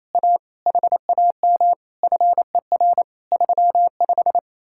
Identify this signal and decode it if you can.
Morse code, heard as a keyed tone, about 37 wpm, 700 Hz. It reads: A HAM FER 35